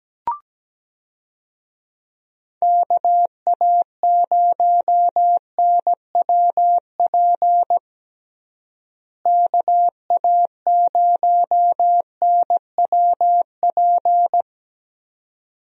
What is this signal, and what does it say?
Morse code, heard as a keyed tone, about 17 wpm, 700 Hz. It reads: KA0NWP KA0NWP
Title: USA Call Signs - 6 Characters - Repeated 2x - 17wpm